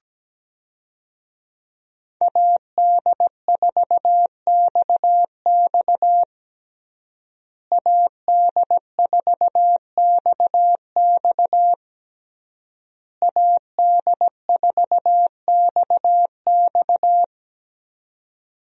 Morse code, heard as a keyed tone, about 17 wpm, 700 Hz. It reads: AD4XX AD4XX AD4XX
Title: USA Call Signs - 5 Characters - Repeated 3x - 17wpm